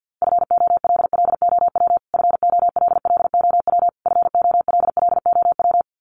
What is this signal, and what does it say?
Morse code, heard as a keyed tone, about 50 wpm, 700 Hz. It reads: FOLLOW FOLLOW FOLLOW
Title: Top 200 Words Repeated 50wpm